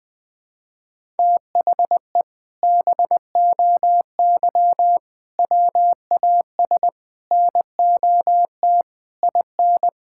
Morse code, heard as a keyed tone, about 20 wpm, 700 Hz. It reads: THE BOY WAS NOT IN
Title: Sentences from Top 1000 Words - Review Part 6 - 20wpm